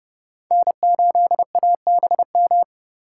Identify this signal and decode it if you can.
Morse code, heard as a keyed tone, about 32 wpm, 700 Hz. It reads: N8ABM
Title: USA Call Signs - 5 Characters - 3x Character Spacing - 30wpm